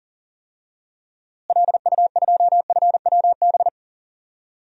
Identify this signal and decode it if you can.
Morse code, heard as a keyed tone, about 40 wpm, 700 Hz. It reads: LU2FWB
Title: DXCC-100 Call Signs - 6 Characters - Repeated 2x - 40wpm